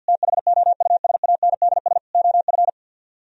Morse code, heard as a keyed tone, about 62 wpm, 700 Hz. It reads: THOUSANDS OF
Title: Speed-Racing - Sets of 4 Words 50wpm